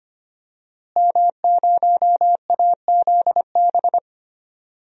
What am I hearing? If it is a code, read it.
Morse code, heard as a keyed tone, about 25 wpm, 700 Hz. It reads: M0AZB
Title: DXCC-100 Call Signs - 5 Characters - Repeated 2x - 25wpm